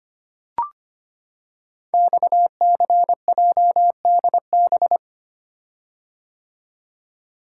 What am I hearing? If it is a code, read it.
Morse code, heard as a keyed tone, about 25 wpm, 700 Hz. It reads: XCJDB
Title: Code Groups - 5 Letters - 25wpm